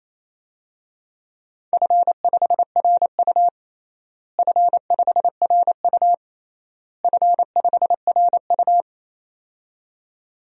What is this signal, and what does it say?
Morse code, heard as a keyed tone, about 28 wpm, 700 Hz. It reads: F5RU F5RU F5RU
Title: DXCC-100 Call Signs - 4 Characters - Repeated 3x - 28wpm